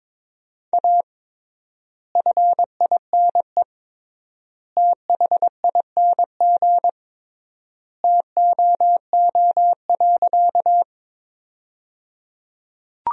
Morse code, heard as a keyed tone, about 22 wpm, 700 Hz. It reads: A FINE THING TOO.
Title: 3x Word-Spacing - Sentences from Top 500 Words - 22wpm